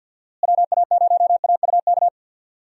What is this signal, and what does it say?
Morse code, heard as a keyed tone, about 46 wpm, 700 Hz. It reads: WA0AUK
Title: USA Call Signs - 6 Characters - Repeated 2x - 50wpm